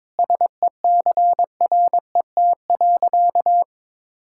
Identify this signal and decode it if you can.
Morse code, heard as a keyed tone, about 22 wpm, 700 Hz. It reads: SECRET.
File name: practice / 22wpm / Sentences with 4-Word Phrases - 22wpm